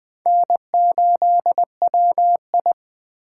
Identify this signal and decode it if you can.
Morse code, heard as a keyed tone, about 20 wpm, 700 Hz. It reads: N8WI